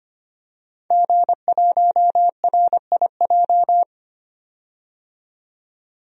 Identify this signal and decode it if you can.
Morse code, heard as a keyed tone, about 25 wpm, 700 Hz. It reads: G1RIJ
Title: DXCC-100 Call Signs - 5 Characters - 25wpm